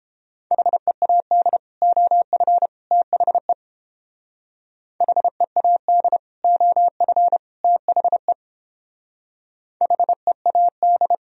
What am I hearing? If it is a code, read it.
Morse code, heard as a keyed tone, about 30 wpm, 700 Hz. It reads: HEAD OF THE HEAD OF THE HEAD